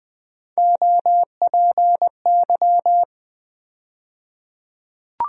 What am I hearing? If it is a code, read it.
Morse code, heard as a keyed tone, about 20 wpm, 700 Hz. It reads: OPY
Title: USA Call Sign Postfixes - 3 Letters 20wpm